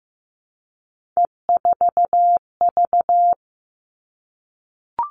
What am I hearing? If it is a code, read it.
Morse code, heard as a keyed tone, about 15 wpm, 700 Hz. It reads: E4V
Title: All Cut Numbers - 3-Digit 15wpm